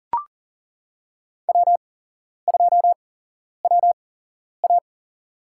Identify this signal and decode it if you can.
Morse code, heard as a keyed tone, about 40 wpm, spaced as for 11 wpm, 700 Hz. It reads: W2WA
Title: USA Call Signs - 4 Characters - 8x Character Spacing - 40wpm